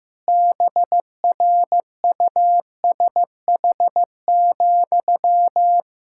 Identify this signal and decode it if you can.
Morse code, heard as a keyed tone, about 15 wpm, 700 Hz. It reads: BRUSH,
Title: Pangram Challenge v2 - 15wpm